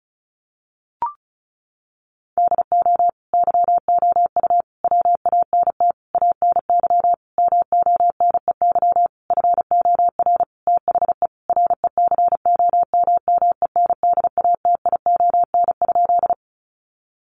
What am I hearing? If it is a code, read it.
Morse code, heard as a keyed tone, about 34 wpm, 700 Hz. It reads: DO YOU WANT ANY MONEY FOR THE RECOMMENDATION?
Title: Sentences from Top 1400 Words 35wpm